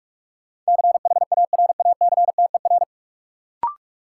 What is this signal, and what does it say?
Morse code, heard as a keyed tone, about 46 wpm, 700 Hz. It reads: CHARACTER
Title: Sentences from Top 1200 Words - New Word Drill - 45wpm